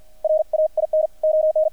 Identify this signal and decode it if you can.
Morse code, heard as a keyed tone, about 50 wpm, 620 Hz. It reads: RAIN ON
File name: archived / CRSnet